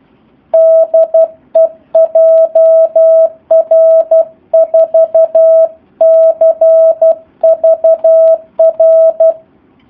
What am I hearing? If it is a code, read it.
Morse code, heard as a keyed tone, about 12 wpm, 650 Hz. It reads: DE JR4CVR
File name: jh8cbh / 2019cw / SP04- w